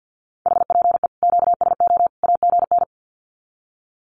Morse code, heard as a keyed tone, about 50 wpm, 700 Hz. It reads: HPE QSO AGN
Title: Common QSO Phrases 50wpm